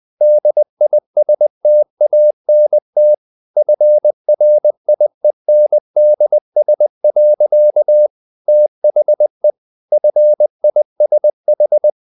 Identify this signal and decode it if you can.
Morse code, heard as a keyed tone, about 20 wpm, 600 Hz. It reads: DISTANT FRIENDS. THE FISH
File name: the old man and the sea part 2 20wpm